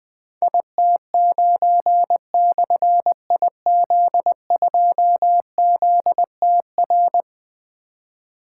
Morse code, heard as a keyed tone, about 20 wpm, 700 Hz. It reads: IT9/IZ2ZTR
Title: Difficult Call Signs 20wpm